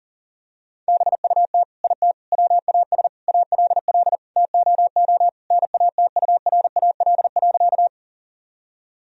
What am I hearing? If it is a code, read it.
Morse code, heard as a keyed tone, about 40 wpm, 700 Hz. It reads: BUT IT WAS ALL TOO NATURAL.
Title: Sentences from Top 1000 Words - Review Part 4 - 40wpm